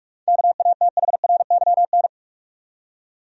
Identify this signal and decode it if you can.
Morse code, heard as a keyed tone, about 46 wpm, 700 Hz. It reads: KATHRYN
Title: USA Common Names - 45wpm